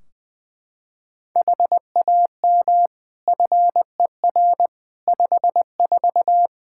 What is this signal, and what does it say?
Morse code, heard as a keyed tone, about 20 wpm, 700 Hz. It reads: HAM FER 54
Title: Practice QSO - Exchange 3 - More Info - 20wpm